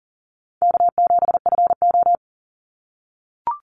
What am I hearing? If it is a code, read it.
Morse code, heard as a keyed tone, about 40 wpm, 700 Hz. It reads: K7FO
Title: NA SOTA Chaser Callsigns 40wpm